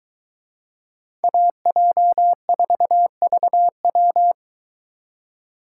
Morse code, heard as a keyed tone, about 23 wpm, 700 Hz. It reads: AJ4VW